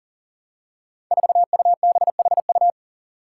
Morse code, heard as a keyed tone, about 40 wpm, 700 Hz. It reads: 4UBHU
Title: Five Challenging Letter-Number Combinations 40wpm